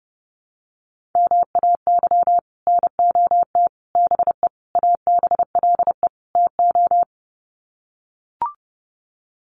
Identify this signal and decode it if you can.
Morse code, heard as a keyed tone, about 30 wpm, 700 Hz. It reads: MAY NOT BE ABLE TO
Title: Sets of 5 Words 30wpm